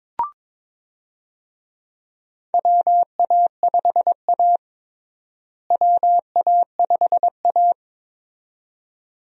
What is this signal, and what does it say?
Morse code, heard as a keyed tone, about 21 wpm, 700 Hz. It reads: WA5A WA5A
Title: USA Call Signs - 4 Characters - Repeated 2x - 22wpm